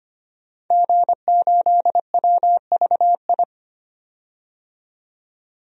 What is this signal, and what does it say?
Morse code, heard as a keyed tone, about 25 wpm, 700 Hz. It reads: G8WVI
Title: DXCC-100 Call Signs - 5 Characters - 25wpm